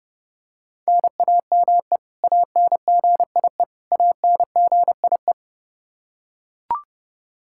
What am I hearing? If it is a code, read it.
Morse code, heard as a keyed tone, about 30 wpm, 700 Hz. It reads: NAME ANGIE ANGIE
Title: Practice QSO - Exchange 1 - Beginning - 30wpm